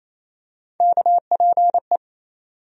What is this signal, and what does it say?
Morse code, heard as a keyed tone, about 27 wpm, 700 Hz. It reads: KPE
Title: USA Call Sign Postfixes - 3 Letters 28wpm